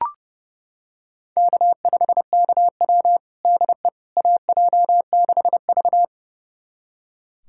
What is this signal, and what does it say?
Morse code, heard as a keyed tone, about 30 wpm, 700 Hz. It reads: K5KW DE AJ6V
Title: USA Callsign DE Callsign - 4 Characters - 30wpm